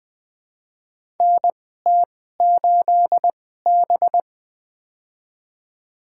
Morse code, heard as a keyed tone, about 20 wpm, spaced as for 14 wpm, 700 Hz. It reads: NT8B
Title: USA Call Signs - 4 Characters - 2x Character Spacing - 20wpm